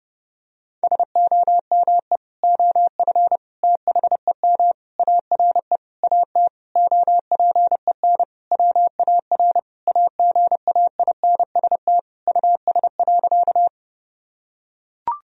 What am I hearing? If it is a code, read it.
Morse code, heard as a keyed tone, about 30 wpm, 700 Hz. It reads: SOME OF THEM ARE AT OPEN WAR AGAINST US.
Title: Sentences from Top 1000 Words - Review Part 2 - 30wpm